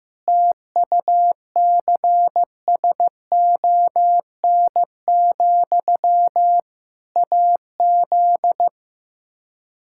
Morse code, heard as a keyed tone, about 15 wpm, 700 Hz. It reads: TUCSON, AZ
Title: K1USN SST QSO Practice 15wpm